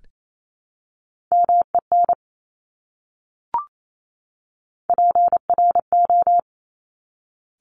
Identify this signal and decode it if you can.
Morse code, heard as a keyed tone, about 28 wpm, 700 Hz. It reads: MEN PRO
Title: Top Three Letter Combinations in Words 28wpm